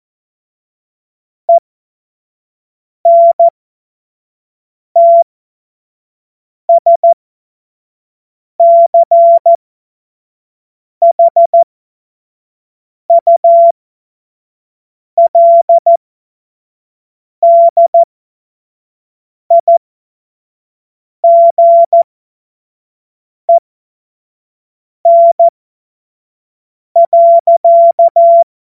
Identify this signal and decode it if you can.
Morse code, heard as a keyed tone, about 14 wpm, spaced as for 5 wpm, 680 Hz. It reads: ENTSCHULDIGEN.
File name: Morsekurs / Witze / Witz 18. April 2023 25 F70